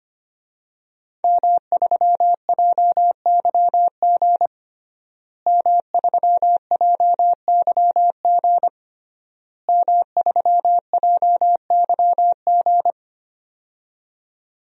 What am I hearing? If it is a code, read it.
Morse code, heard as a keyed tone, about 25 wpm, 700 Hz. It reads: M3JYG M3JYG M3JYG
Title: DXCC-100 Call Signs - 5 Characters - Repeated 3x - 25wpm